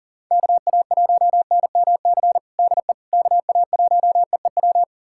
Morse code, heard as a keyed tone, about 37 wpm, 700 Hz. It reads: KA1NMC DE KA1EEW
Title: USA Callsign DE Callsign 40wpm